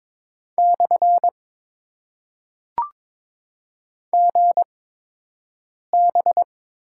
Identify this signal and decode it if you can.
Morse code, heard as a keyed tone, about 22 wpm, spaced as for 6 wpm, 700 Hz. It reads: / GB
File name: Top 90 QSO Elements - 8x Character Spacing - 22wpm